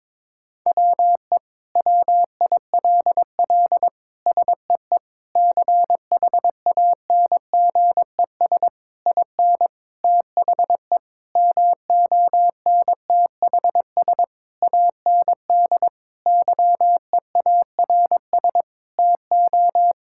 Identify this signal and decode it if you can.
Morse code, heard as a keyed tone, about 22 wpm, 700 Hz. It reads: WE WILL SEE CHANGES IN THE MONTHS AND YEARS TO
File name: Sentences with 7-Word Phrases - 22wpm